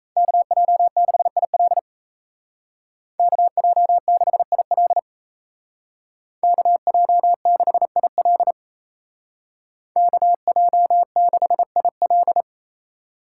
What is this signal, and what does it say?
Morse code, heard as a keyed tone, about 36 wpm, 700 Hz. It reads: KJ6IL KJ6IL KJ6IL KJ6IL